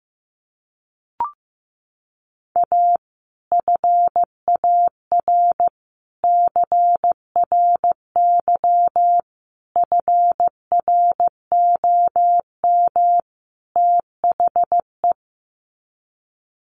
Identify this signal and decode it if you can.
Morse code, heard as a keyed tone, about 15 wpm, 700 Hz. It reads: A FAR CRY FROM THE
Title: Sets of 5 Words Encore 15wpm